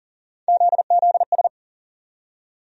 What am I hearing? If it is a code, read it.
Morse code, heard as a keyed tone, about 40 wpm, 700 Hz. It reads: ZZS
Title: USA Call Sign Postfixes - 3 Letters 40wpm